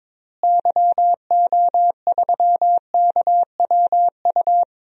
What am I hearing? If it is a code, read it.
Morse code, heard as a keyed tone, about 22 wpm, 700 Hz. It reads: YO3KWU
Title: DXCC-100 Call Signs - 6 Characters - Repeated 2x - 22wpm